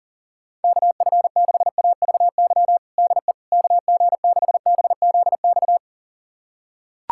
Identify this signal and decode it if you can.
Morse code, heard as a keyed tone, about 37 wpm, 700 Hz. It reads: KF6AVY DE KG6BZX
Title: USA Callsign DE Callsign - 6 Characters - 40wpm